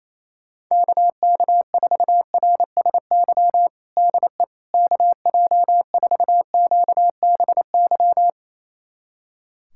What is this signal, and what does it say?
Morse code, heard as a keyed tone, about 28 wpm, 700 Hz. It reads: KK4RSY DE KJ4QBY
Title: USA Callsign DE Callsign - 6 Characters - 28wpm